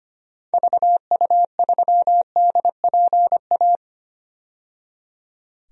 Morse code, heard as a keyed tone, about 25 wpm, 700 Hz. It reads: VU3DPA
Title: DXCC-100 Call Signs - 6 Characters - 25wpm